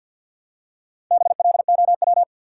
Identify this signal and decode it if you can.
Morse code, heard as a keyed tone, about 47 wpm, 700 Hz. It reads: BLOW
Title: Four Letter Words - 50wpm